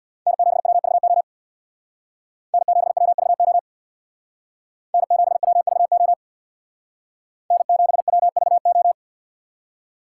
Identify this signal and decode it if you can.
Morse code, heard as a keyed tone, about 63 wpm, 700 Hz. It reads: N7WVO N7WVO N7WVO N7WVO